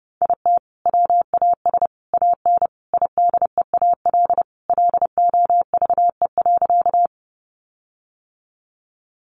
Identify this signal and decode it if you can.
Morse code, heard as a keyed tone, about 30 wpm, 700 Hz. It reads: IT WAS AN IDEAL LOVE.
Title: Sentences from Top 1400 Words 30wpm